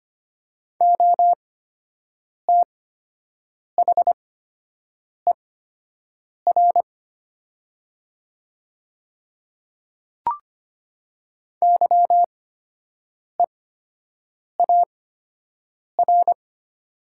Morse code, heard as a keyed tone, about 25 wpm, spaced as for 7 wpm, 700 Hz. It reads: OTHER YEAR
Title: 8x Character Spacing - Top 200 Words - 25wpm